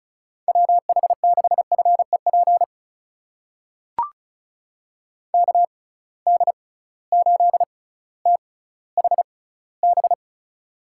Morse code, heard as a keyed tone, about 36 wpm, spaced as for 18 wpm, 700 Hz. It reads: WH6FEP KD8THB
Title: USA Call Signs - 6 Characters - 6x Character Spacing - 35wpm